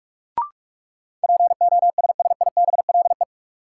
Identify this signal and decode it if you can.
Morse code, heard as a keyed tone, about 46 wpm, 700 Hz. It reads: POSSIBLE
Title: Sentences from Top 500 Words - New Word Drill - 45wpm